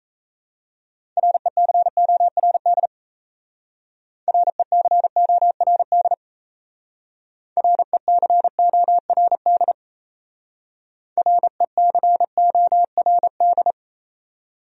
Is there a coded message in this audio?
Yes: Morse code from 1.2 to 13.7 s, heard as a keyed tone, about 36 wpm, 700 Hz. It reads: RECORD RECORD RECORD RECORD